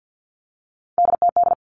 Morse code, heard as a keyed tone, about 50 wpm, 700 Hz. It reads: DTD
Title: Three Challenging Letter-Number Combinations 50wpm